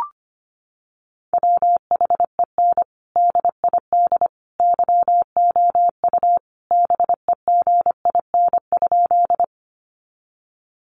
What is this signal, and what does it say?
Morse code, heard as a keyed tone, about 25 wpm, 700 Hz. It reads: WHEN DID YOU BEGIN?
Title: Sentences from Top 300 Words 25wpm